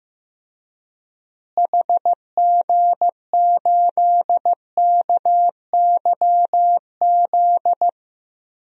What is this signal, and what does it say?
Morse code, heard as a keyed tone, about 15 wpm, 700 Hz. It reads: HG8KYZ
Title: DXCC-100 Call Signs - 6 Characters - Repeated 2x - 15wpm